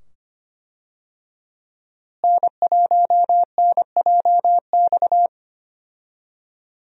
Morse code, heard as a keyed tone, about 25 wpm, 700 Hz. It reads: N1NJX
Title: USA Call Signs - 5 Characters - Repeated 2x - 25wpm